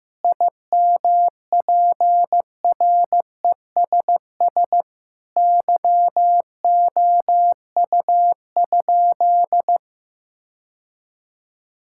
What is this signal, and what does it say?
Morse code, heard as a keyed tone, about 15 wpm, 700 Hz. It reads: IMPRESS YOU?
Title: Sentences from Top 1600 Words 15wpm